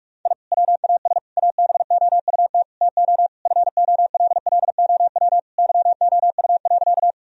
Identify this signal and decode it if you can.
Morse code, heard as a keyed tone, about 45 wpm, 700 Hz. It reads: I WAS ABOUT TO FOLLOW YOU.
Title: Sentences from Top 1000 Words - Review Part 1 - 45wpm